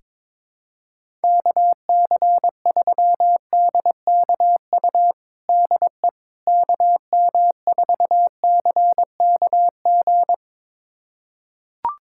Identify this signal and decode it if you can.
Morse code, heard as a keyed tone, about 21 wpm, 700 Hz. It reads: KC3DKU DE KM4CKG
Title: USA Callsign DE Callsign 22wpm